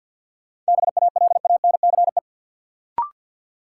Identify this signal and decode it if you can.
Morse code, heard as a keyed tone, about 51 wpm, 700 Hz. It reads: BALANCE
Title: Sentences from Top 900 Words - New Word Drill - 50wpm